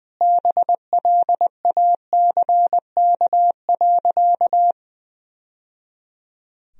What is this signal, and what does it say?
Morse code, heard as a keyed tone, about 20 wpm, 700 Hz. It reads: BLACK.
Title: Sentences from Top 1000 Words - Review Part 2 - 20wpm